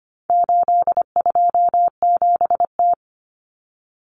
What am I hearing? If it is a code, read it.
Morse code, heard as a keyed tone, about 25 wpm, 700 Hz. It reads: 827T
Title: Common Cut Numbers - 4-Digit 25wpm